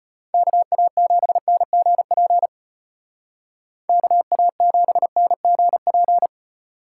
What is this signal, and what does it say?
Morse code, heard as a keyed tone, about 31 wpm, 700 Hz. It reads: KA7NGP KA7NGP
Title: Speed Racing - USA 2x3 Call Signs 25wpm